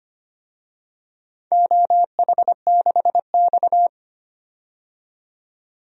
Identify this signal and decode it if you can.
Morse code, heard as a keyed tone, about 25 wpm, 700 Hz. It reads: OH6X
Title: DXCC-100 Call Signs - 4 Characters - 25wpm